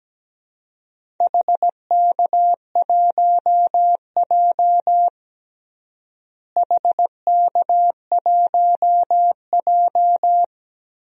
Morse code, heard as a keyed tone, about 17 wpm, 700 Hz. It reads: HK1J HK1J
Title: DXCC-100 Call Signs - 4 Characters - Repeated 2x - 17wpm